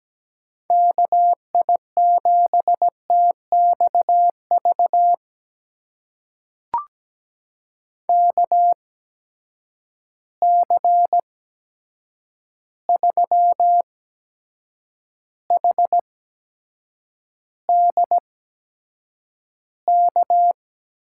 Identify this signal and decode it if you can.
Morse code, heard as a keyed tone, about 17 wpm, spaced as for 7 wpm, 700 Hz. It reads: KI7TXV KC3HDK
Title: USA Call Signs - 6 Characters - 8x Character Spacing - 17wpm